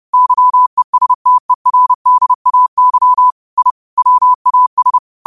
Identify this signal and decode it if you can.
Morse code, heard as a keyed tone, about 30 wpm, 1000 Hz. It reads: YESTERDAY I WAS
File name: qotdfiles / 30wpm